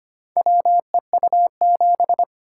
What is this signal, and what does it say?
Morse code, heard as a keyed tone, about 25 wpm, 700 Hz. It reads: WEU7
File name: practice / 25wpm / Four Letter-Number Combinations - 25wpm